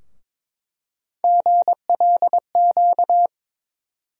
Morse code, heard as a keyed tone, about 21 wpm, 700 Hz. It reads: GLQ
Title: USA Call Sign Postfixes - 3 Letters 22wpm